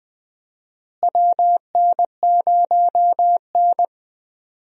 Morse code, heard as a keyed tone, about 20 wpm, 700 Hz. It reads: WN0N
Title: USA Call Signs - 4 Characters - Repeated 2x - 20wpm